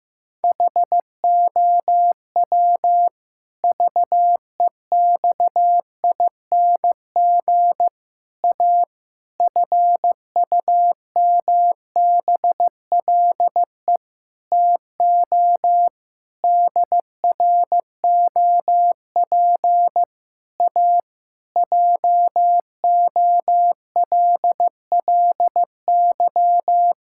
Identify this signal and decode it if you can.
Morse code, heard as a keyed tone, about 15 wpm, 700 Hz. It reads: HOW VEXING A FUMBLE TO DROP A JOLLY
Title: Pangram Challenge v2 - 15wpm